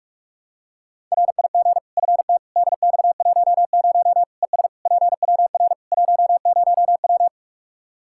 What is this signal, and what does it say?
Morse code, heard as a keyed tone, about 46 wpm, 700 Hz. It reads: RIG FT DX10 ES PWR 10W
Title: Practice QSO - Exchange 2 - Info - 45wpm